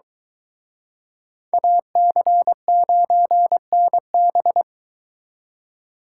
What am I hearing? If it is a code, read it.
Morse code, heard as a keyed tone, about 23 wpm, 700 Hz. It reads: AC9NB